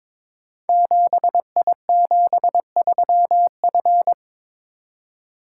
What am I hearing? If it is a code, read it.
Morse code, heard as a keyed tone, about 22 wpm, 700 Hz. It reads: 7I73F
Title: Code Groups - 5 Letter:Numbers - 22wpm